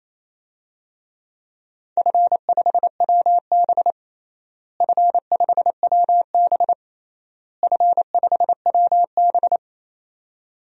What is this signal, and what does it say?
Morse code, heard as a keyed tone, about 28 wpm, 700 Hz. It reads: F5WB F5WB F5WB
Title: DXCC-100 Call Signs - 4 Characters - Repeated 3x - 28wpm